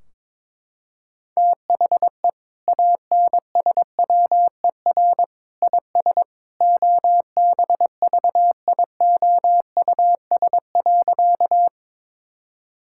Morse code, heard as a keyed tone, about 22 wpm, 700 Hz. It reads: THE ANSWER IS OBVIOUS.
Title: Sentences from Top 1600 Words 22wpm